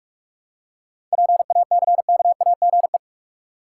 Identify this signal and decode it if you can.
Morse code, heard as a keyed tone, about 47 wpm, 700 Hz. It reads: PACKAGE
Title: Seven Letter Words - 45wpm